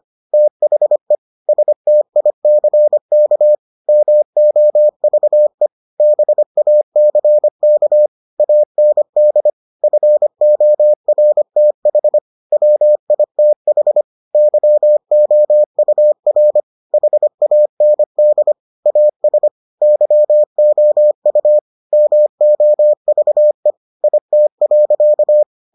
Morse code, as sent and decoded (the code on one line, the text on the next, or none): - .... / ... - .. -.-. -.- / -- --- ...- . / -... .- -.-. -.- / .- -. -.. / ..-. --- .-. - .... / .-- .. - .... / -.-- --- ..- .-. / .... .- -. -.. / .- ... / -.-- --- ..- / -- --- ...- . / .. - .-.-.-
THE STICK MOVE BACK AND FORTH WITH YOUR HAND AS YOU MOVE IT.